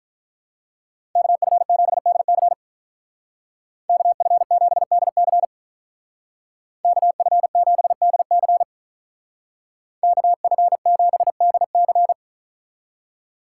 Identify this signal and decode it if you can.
Morse code, heard as a keyed tone, about 43 wpm, 700 Hz. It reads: KF7DC KF7DC KF7DC KF7DC
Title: Speed Racing - USA 2x2 Call Signs 35wpm